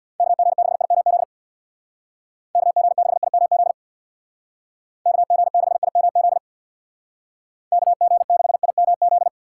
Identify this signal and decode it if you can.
Morse code, heard as a keyed tone, about 62 wpm, 700 Hz. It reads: KG6IMZ KG6IMZ KG6IMZ KG6IMZ
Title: Speed Racing - USA 2x3 Call Signs 50wpm